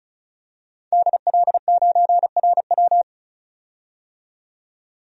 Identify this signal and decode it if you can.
Morse code, heard as a keyed tone, about 36 wpm, 700 Hz. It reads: DL9RW
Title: DXCC-100 Call Signs - 5 Characters - 35wpm